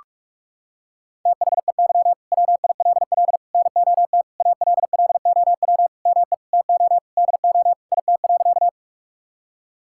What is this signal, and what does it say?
Morse code, heard as a keyed tone, about 41 wpm, 700 Hz. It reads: THEY WILL NOT ALLOW ME TO DO IT.
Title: Sentences from Top 1000 Words - Review Part 5 - 45wpm